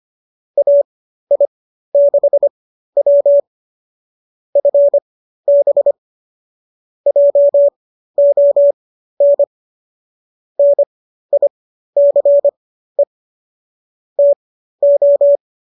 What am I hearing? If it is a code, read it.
Morse code, heard as a keyed tone, about 25 wpm, spaced as for 13 wpm, 570 Hz. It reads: AI6W FB JON NICE TO
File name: qso207 13